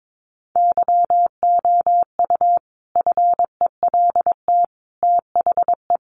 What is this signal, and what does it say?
Morse code, heard as a keyed tone, about 22 wpm, 700 Hz. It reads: YOU FELT THE
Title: A Sound Of Thunder 22wpm